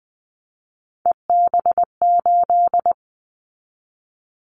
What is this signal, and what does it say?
Morse code, heard as a keyed tone, about 19 wpm, 700 Hz. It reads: EB8
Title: Callsign Prefixes for Top 100 DXCC - 20wpm